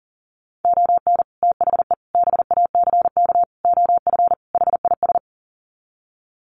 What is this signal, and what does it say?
Morse code, heard as a keyed tone, about 40 wpm, 700 Hz. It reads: ON THE BACK OF HIS
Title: Sets of 5 Words Encore 40wpm